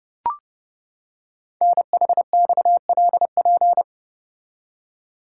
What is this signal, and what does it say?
Morse code, heard as a keyed tone, about 28 wpm, 700 Hz. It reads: NHXLP
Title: Code Groups - 5 Letter:Numbers - 30wpm